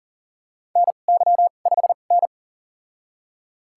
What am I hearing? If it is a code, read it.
Morse code, heard as a keyed tone, about 40 wpm, spaced as for 29 wpm, 700 Hz. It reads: NY5N
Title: USA Call Signs - 4 Characters - 2x Character Spacing - 40wpm